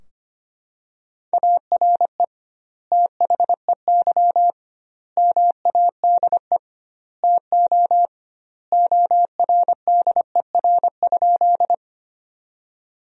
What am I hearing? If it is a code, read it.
Morse code, heard as a keyed tone, about 25 wpm, 700 Hz. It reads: ARE THEY MADE TO ORDER?
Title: 2x Word-Spacing - Sentences from Top 400 Words - 25wpm